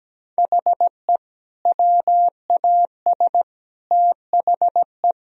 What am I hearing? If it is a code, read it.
Morse code, heard as a keyed tone, about 17 wpm, 700 Hz. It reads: HE WAS THE
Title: Sentences from Top 1000 Words - Review Part 3 - 17wpm